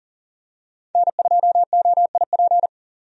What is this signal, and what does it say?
Morse code, heard as a keyed tone, about 40 wpm, 700 Hz. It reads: N2OIP
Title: USA Call Signs - 5 Characters - 40wpm